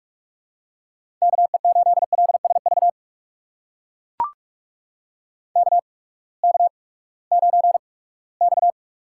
Morse code, heard as a keyed tone, about 44 wpm, spaced as for 22 wpm, 700 Hz. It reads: KE8LSV KK9X
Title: USA Call Signs - 8x Character Spacing - 45wpm